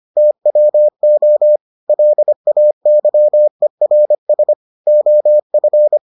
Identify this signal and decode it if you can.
Morse code, heard as a keyed tone, about 25 wpm, 600 Hz. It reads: TWO LAYERS OF